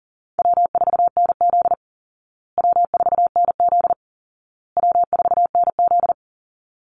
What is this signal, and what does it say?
Morse code, heard as a keyed tone, about 37 wpm, 700 Hz. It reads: W4NZ W4NZ W4NZ
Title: Call Signs Repeated 2x - 40wpm